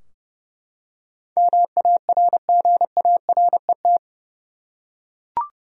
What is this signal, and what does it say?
Morse code, heard as a keyed tone, about 30 wpm, 700 Hz. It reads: MARGARET
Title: USA Common Names - 30wpm